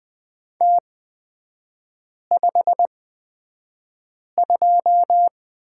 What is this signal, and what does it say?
Morse code, heard as a keyed tone, about 20 wpm, spaced as for 5 wpm, 700 Hz. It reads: T52
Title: 255 Lesson 41e - Period - Three Characters 20wpm